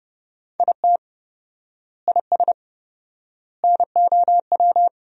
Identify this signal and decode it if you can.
Morse code, heard as a keyed tone, about 30 wpm, 700 Hz. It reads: IT IS NOW
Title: 4x Word-Spacing - Sentences from Top 200 Words - 30wpm